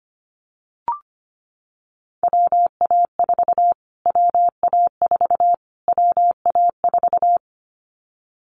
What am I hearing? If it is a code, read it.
Morse code, heard as a keyed tone, about 25 wpm, 700 Hz. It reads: WA4 WA4 WA4
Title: Callsign Prefixes for Top 25 DXCC - Repeated 3x - 25wpm